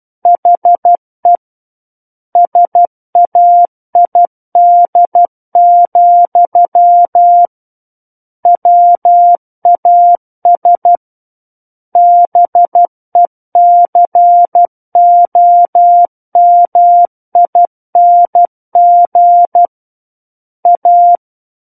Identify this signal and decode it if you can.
Morse code, heard as a keyed tone, about 12 wpm, 700 Hz. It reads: HE SAID, WAS BECOMING A
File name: Book One / 03-Chapter 3 - 12 WPM